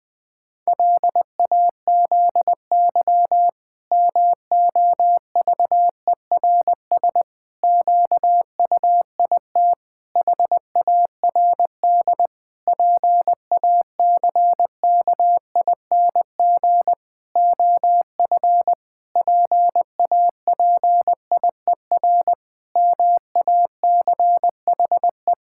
Morse code, heard as a keyed tone, about 20 wpm, 700 Hz. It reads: LAZY MOVERS QUIT HARD PACKING OF PAPIER MACHE
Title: Pangram Challenge v2 - 20wpm